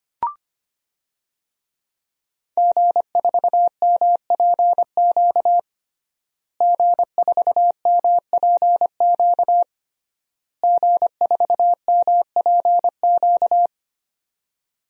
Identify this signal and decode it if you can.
Morse code, heard as a keyed tone, about 25 wpm, 700 Hz. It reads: G4MPQ G4MPQ G4MPQ
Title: DXCC-100 Call Signs - 5 Characters - Repeated 3x - 25wpm